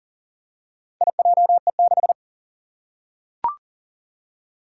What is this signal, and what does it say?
Morse code, heard as a keyed tone, about 40 wpm, 700 Hz. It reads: IJE6
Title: Four Challenging Letter-Number Combinations 40wpm